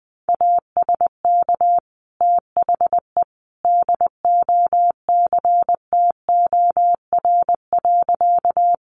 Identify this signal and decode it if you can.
Morse code, heard as a keyed tone, about 20 wpm, 700 Hz. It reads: ASK THE DOCTOR.
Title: Sentences from Top 1500 Words 20wpm